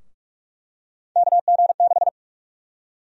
Morse code, heard as a keyed tone, about 46 wpm, 700 Hz. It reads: KG6
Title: USA Call Sign Prefixes - Repeated 2x - 45wpm